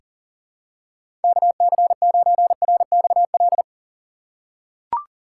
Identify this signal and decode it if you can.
Morse code, heard as a keyed tone, about 37 wpm, 700 Hz. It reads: KC9RXL
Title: USA Call Signs - 6x Character Spacing - 40wpm